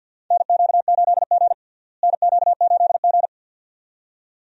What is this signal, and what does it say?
Morse code, heard as a keyed tone, about 47 wpm, 700 Hz. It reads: NQ8G NQ8G
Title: USA Call Signs - 4 Characters - Repeated 2x - 50wpm